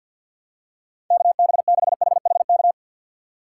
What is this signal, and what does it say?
Morse code, heard as a keyed tone, about 53 wpm, 700 Hz. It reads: KB6HHK
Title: USA Call Signs - 6 Characters - 4x Character Spacing - 50wpm